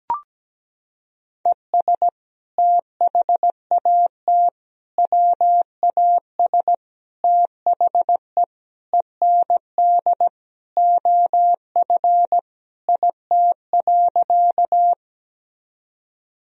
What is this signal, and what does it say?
Morse code, heard as a keyed tone, about 17 wpm, 700 Hz. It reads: ES THAT WAS THE END OF IT.
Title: Sentences from Top 200 Words and Abbreviations 17wpm